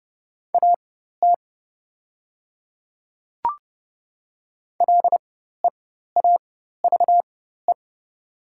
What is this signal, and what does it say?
Morse code, heard as a keyed tone, about 30 wpm, spaced as for 14 wpm, 690 Hz. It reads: AT LEAVE
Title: 4x Character Spacing - Top 500 Words - 30wpm